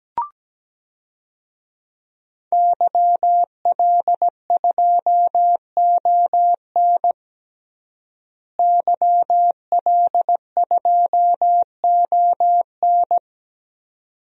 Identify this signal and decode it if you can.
Morse code, heard as a keyed tone, about 17 wpm, 700 Hz. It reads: YL2ON YL2ON
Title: DXCC-100 Call Signs - 5 Characters - Repeated 2x - 17wpm